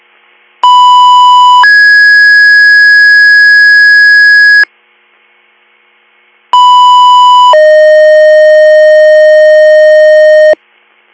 Two-tone page, heard as a one-second tone, then a three-second tone, about 990 then 1650 Hz, about 990 then 620 Hz. It is heard from 0.6 to 10.5 s.